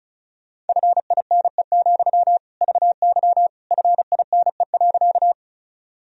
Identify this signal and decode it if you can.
Morse code, heard as a keyed tone, about 36 wpm, 700 Hz. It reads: FINE, VY FINE.
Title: Sentences from Top 500 Words and Abbreviations 35wpm